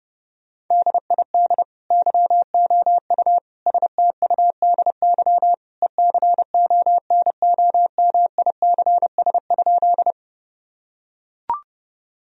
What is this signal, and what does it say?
Morse code, heard as a keyed tone, about 30 wpm, 700 Hz. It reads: DID YOU STUDY ECONOMICS?
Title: Sentences from Top 1000 Words - Review Part 3 - 30wpm